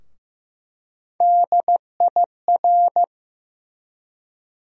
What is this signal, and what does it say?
Morse code, heard as a keyed tone, about 15 wpm, 700 Hz. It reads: DIR
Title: USA Call Sign Postfixes - 3 Letters 15wpm